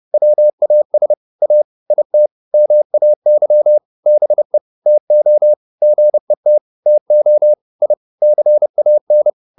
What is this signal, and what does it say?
Morse code, heard as a keyed tone, about 30 wpm, 600 Hz. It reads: WAS A IT MAY BE TO GET TO I CAN
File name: WD201 30